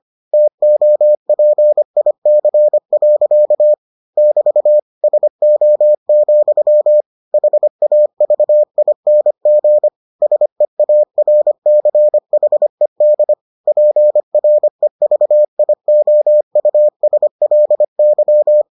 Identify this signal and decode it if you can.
Morse code, heard as a keyed tone, about 25 wpm, 600 Hz. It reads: TOPIC. <BT> SO, HAVING SEARCHED PREVIOUSLY